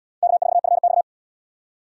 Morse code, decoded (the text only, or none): K4FX